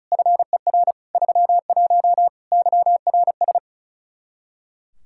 Morse code, heard as a keyed tone, about 34 wpm, 700 Hz. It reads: FER 31 YRS
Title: Practice QSO - Exchange 3 - More Info - 35wpm